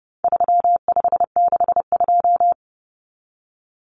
Morse code, heard as a keyed tone, about 29 wpm, 700 Hz. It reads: 3562
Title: Numbers - 4-Digit 30wpm